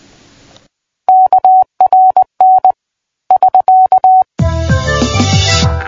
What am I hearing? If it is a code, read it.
Morse code, heard as a keyed tone, about 20 wpm, 750 Hz. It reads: KRN <SK>
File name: PI4AA Uitzending